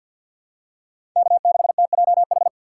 Morse code, heard as a keyed tone, about 50 wpm, 700 Hz. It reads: K6TJH
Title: USA Call Signs - 6x Character Spacing - 50wpm